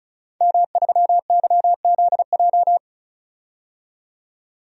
Morse code, heard as a keyed tone, about 36 wpm, 700 Hz. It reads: M3YZJ
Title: DXCC-100 Call Signs - 5 Characters - 35wpm